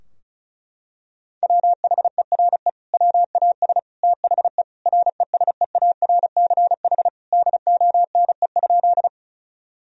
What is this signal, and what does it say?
Morse code, heard as a keyed tone, about 34 wpm, 700 Hz. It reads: WHERE WAS THE RESEARCH DONE?
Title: Sentences from Top 1000 Words - Review Part 4 - 35wpm